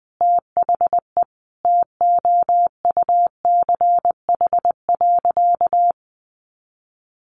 Morse code, heard as a keyed tone, about 20 wpm, 700 Hz. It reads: THE TOUCH.
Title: Sentences from Top 1500 Words 20wpm